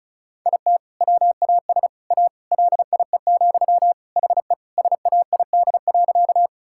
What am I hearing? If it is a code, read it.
Morse code, heard as a keyed tone, about 36 wpm, 700 Hz. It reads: IT WAS A LIE, HE SAID.
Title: Sentences from Top 1600 Words 35wpm